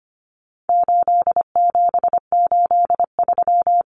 Morse code, heard as a keyed tone, about 25 wpm, 700 Hz. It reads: 8783
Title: Numbers - 4-Digit 25wpm